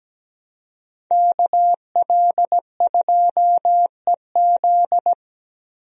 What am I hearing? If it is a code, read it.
Morse code, heard as a keyed tone, about 17 wpm, 700 Hz. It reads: KL2EZ
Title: USA Call Signs - 5 Characters - Repeated 3x - 17wpm